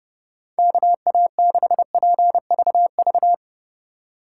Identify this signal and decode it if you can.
Morse code, heard as a keyed tone, about 31 wpm, 700 Hz. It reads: KA6PVV
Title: USA Call Signs - 3x Character Spacing - 30wpm